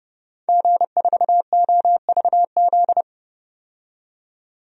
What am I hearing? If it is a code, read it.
Morse code, heard as a keyed tone, about 30 wpm, 690 Hz. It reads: G4OVZ
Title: DXCC-100 Call Signs - 5 Characters - 30wpm